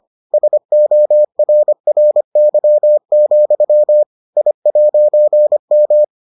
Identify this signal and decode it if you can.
Morse code, heard as a keyed tone, about 25 wpm, 600 Hz. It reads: SORRY, I'M